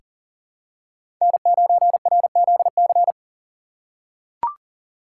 Morse code, heard as a keyed tone, about 40 wpm, 700 Hz. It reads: N9RZC
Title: USA Call Signs - 5 Characters - 2x Character Spacing - 40wpm